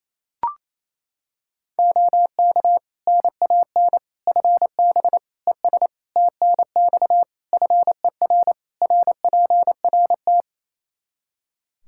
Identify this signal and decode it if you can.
Morse code, heard as a keyed tone, about 28 wpm, 700 Hz. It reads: OK NAN FB ES TNX FER RPRT
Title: Practice QSO - Exchange 2 - Info - 28wpm